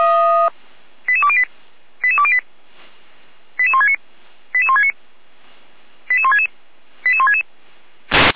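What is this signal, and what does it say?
Five-tone selective call (ZVEI), heard as quick runs of five tones, tones of about 70 ms. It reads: 88208 88208 88169 88169 88166 88166
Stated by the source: ZVEI five-tone call 88352 received by SDR.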